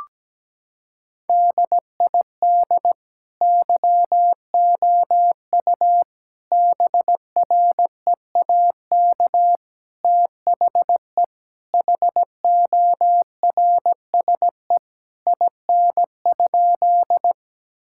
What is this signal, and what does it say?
Morse code, heard as a keyed tone, about 17 wpm, 700 Hz. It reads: DID YOU BREAK THE HORSE IN?
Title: Sentences from Top 1000 Words - Review Part 6 - 17wpm